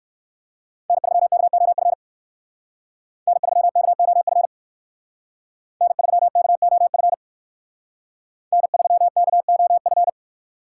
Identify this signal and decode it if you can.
Morse code, heard as a keyed tone, about 56 wpm, 700 Hz. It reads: N3KOF N3KOF N3KOF N3KOF
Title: Speed Racing - USA 1x3 Call Signs 45wpm